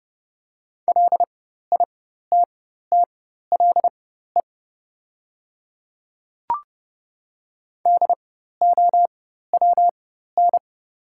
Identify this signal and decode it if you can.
Morse code, heard as a keyed tone, about 30 wpm, spaced as for 14 wpm, 700 Hz. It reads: LITTLE DOWN
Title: 4x Character Spacing - Top 300 Words - 30wpm